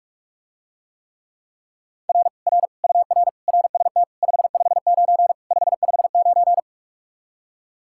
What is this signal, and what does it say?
Morse code, heard as a keyed tone, about 49 wpm, 700 Hz. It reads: R R UR RST 559 559